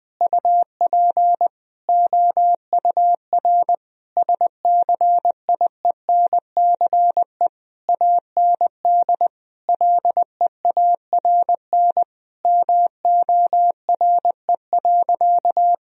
Morse code, heard as a keyed tone, about 20 wpm, 700 Hz. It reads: UP OUR SCIENCE AND LEARN MORE.
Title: Sentences from Top 1000 Words - Review Part 4 - 20wpm